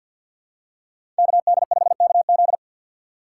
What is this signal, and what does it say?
Morse code, heard as a keyed tone, about 53 wpm, 700 Hz. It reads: KD5KZ
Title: USA Call Signs - 5 Characters - 4x Character Spacing - 50wpm